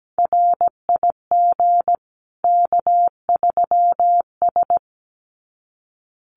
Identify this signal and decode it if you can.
Morse code, heard as a keyed tone, about 17 wpm, 700 Hz. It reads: RIG K3S
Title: Common QSO Phrases 17wpm